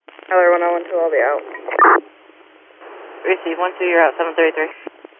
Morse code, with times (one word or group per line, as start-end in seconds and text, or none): none